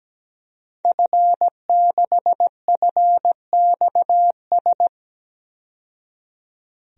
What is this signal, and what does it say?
Morse code, heard as a keyed tone, about 17 wpm, 700 Hz. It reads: F6FXS
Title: DXCC-100 Call Signs - 5 Characters - 17wpm